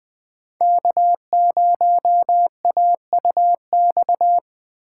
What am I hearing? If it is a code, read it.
Morse code, heard as a keyed tone, about 22 wpm, 700 Hz. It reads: K0AUX